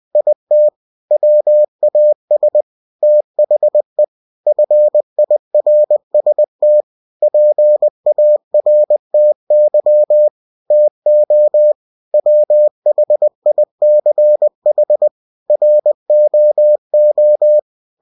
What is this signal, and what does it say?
Morse code, heard as a keyed tone, about 20 wpm, 600 Hz. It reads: IT WAS THE FIRST PARTY TO WHICH ROO
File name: cw-books / winnie-the-pooh / winnie-20wpm-page-0106